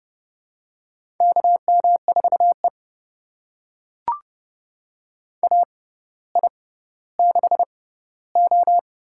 Morse code, heard as a keyed tone, about 30 wpm, spaced as for 15 wpm, 700 Hz. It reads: KM4E AI6O
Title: USA Call Signs - 4 Characters - 6x Character Spacing - 30wpm